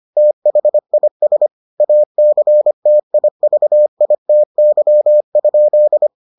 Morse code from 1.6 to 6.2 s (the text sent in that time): ACTIVITY?